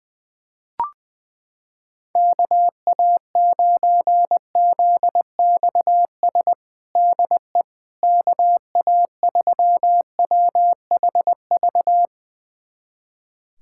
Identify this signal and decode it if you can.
Morse code, heard as a keyed tone, about 20 wpm, 700 Hz. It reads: KA9ZXS DE KA3WHV
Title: USA Callsign DE Callsign - 6 Characters - 20wpm